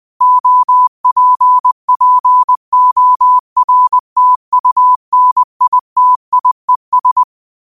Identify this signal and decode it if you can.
Morse code, heard as a keyed tone, about 20 wpm, 1000 Hz. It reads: OPPORTUNITIES